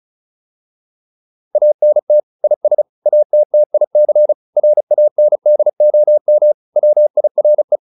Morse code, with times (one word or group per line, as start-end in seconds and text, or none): 1.6-2.2 ANT
2.4-2.8 IS
3.1-4.3 ATTIC
4.6-6.5 RANDOM
6.8-7.8 WIRE